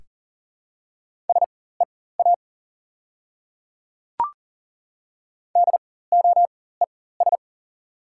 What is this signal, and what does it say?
Morse code, heard as a keyed tone, about 38 wpm, spaced as for 19 wpm, 690 Hz. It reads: SEA DOES
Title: 4x Character Spacing - Top 300 Words - 40wpm